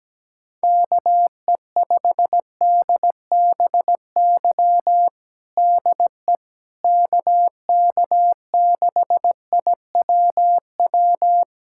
Morse code, heard as a keyed tone, about 17 wpm, 700 Hz. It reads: KE5DBY DE KK6IWW
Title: USA Callsign DE Callsign - 6 Characters - 17wpm